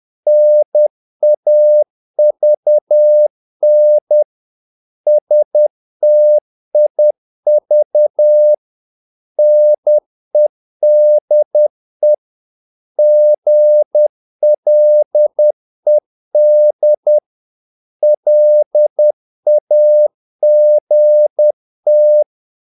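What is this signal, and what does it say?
Morse code, heard as a keyed tone, about 10 wpm, 600 Hz. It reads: NAVN STIV NEDE GLED LAGT
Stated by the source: Morse practice of short Danish words at 10 wpm, from CW med Gnister.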